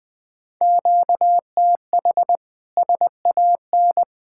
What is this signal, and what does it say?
Morse code, heard as a keyed tone, about 20 wpm, 700 Hz. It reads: QTH SAN
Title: Practice QSO - Exchange 1 - Beginning - 20wpm